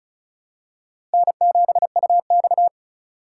Morse code, heard as a keyed tone, about 36 wpm, 700 Hz. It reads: N7UX
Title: USA Call Signs - 4 Characters - 3x Character Spacing - 35wpm